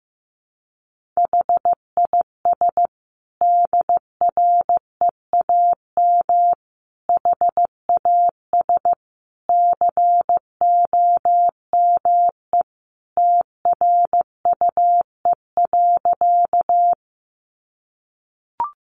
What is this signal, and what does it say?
Morse code, heard as a keyed tone, about 14 wpm, 700 Hz. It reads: HIS DREAM HAS COME TRUE.
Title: Sentences from Top 1500 Words 15wpm